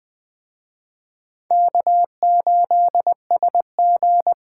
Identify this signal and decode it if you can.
Morse code, heard as a keyed tone, about 20 wpm, 700 Hz. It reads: K8SG
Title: Speed Racing - USA 1x2 Call Signs 17wpm